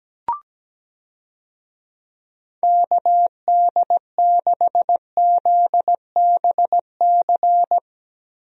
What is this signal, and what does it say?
Morse code, heard as a keyed tone, about 17 wpm, 700 Hz. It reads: KD6ZBC
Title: USA Call Signs - 6 Characters - Repeated 3x - 17wpm